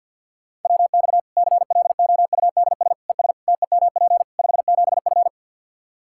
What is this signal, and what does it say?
Morse code, heard as a keyed tone, about 51 wpm, 700 Hz. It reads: WX CLOUDS ES TEMP 57F
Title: Practice QSO - Exchange 2 - Info - 50wpm